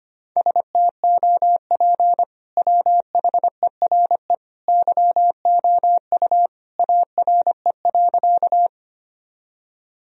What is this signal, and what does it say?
Morse code, heard as a keyed tone, about 25 wpm, 700 Hz. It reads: STOP WHERE YOU ARE.
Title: Sentences from Top 1000 Words - Review Part 1 - 25wpm